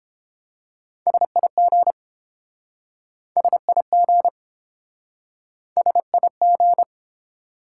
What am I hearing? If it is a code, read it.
Morse code, heard as a keyed tone, about 29 wpm, 700 Hz. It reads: SIG SIG SIG